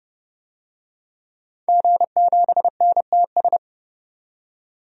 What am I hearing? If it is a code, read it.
Morse code, heard as a keyed tone, about 30 wpm, 700 Hz. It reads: G7NTS
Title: DXCC-100 Call Signs - 5 Characters - 30wpm